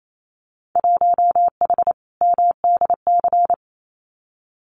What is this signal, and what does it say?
Morse code, heard as a keyed tone, about 28 wpm, 700 Hz. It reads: 1H MDC
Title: Winter Field Day Exchanges 28wpm